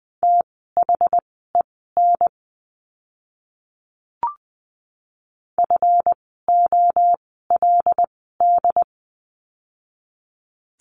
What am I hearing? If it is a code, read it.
Morse code, heard as a keyed tone, about 19 wpm, spaced as for 14 wpm, 700 Hz. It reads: THEN FOLD
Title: 139 Lesson 23f - Letter f - Words 20wpm